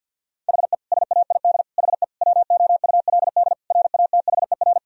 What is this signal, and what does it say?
Morse code, heard as a keyed tone, about 50 wpm, 700 Hz. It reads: HE SAID HE WOULD RATHER
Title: Sentences from Top 1000 Words - Review Part 4 - 50wpm